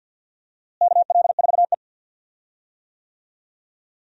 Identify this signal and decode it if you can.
Morse code, heard as a keyed tone, about 47 wpm, 700 Hz. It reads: KL4E
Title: USA Call Signs - 4 Characters - Repeated 2x - 50wpm